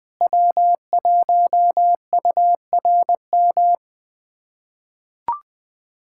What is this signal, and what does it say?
Morse code, heard as a keyed tone, about 21 wpm, 700 Hz. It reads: W1URM
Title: USA Callsign DE Callsign - 5 Characters - 20wpm